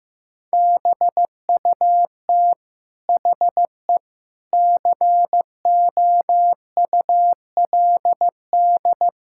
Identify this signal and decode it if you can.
Morse code, heard as a keyed tone, about 15 wpm, 700 Hz. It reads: BUT HE COULD
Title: Sentences from Top 1600 Words 15wpm